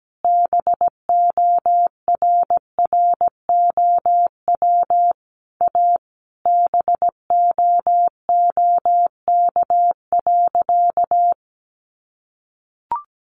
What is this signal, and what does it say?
Morse code, heard as a keyed tone, about 17 wpm, 700 Hz. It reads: BORROW A BOOK.
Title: Sentences from Top 1500 Words 17wpm